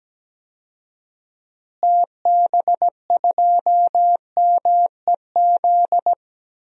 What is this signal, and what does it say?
Morse code, heard as a keyed tone, about 17 wpm, 700 Hz. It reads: TB2MEZ
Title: DXCC-100 Call Signs - 6 Characters - Repeated 3x - 17wpm